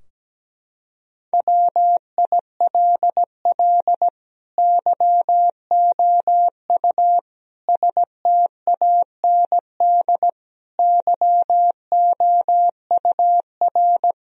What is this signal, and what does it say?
Morse code, heard as a keyed tone, about 17 wpm, 700 Hz. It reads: WILL YOU STAND YOUR
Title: Sentences from Top 1000 Words - Review Part 2 - 17wpm